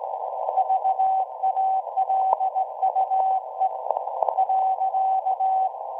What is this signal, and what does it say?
Morse code, heard as a keyed tone, about 17 wpm, 760 Hz. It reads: VALUE .
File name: WebSDR records / SAQ at PI4THT part od 1005z 17-2kHz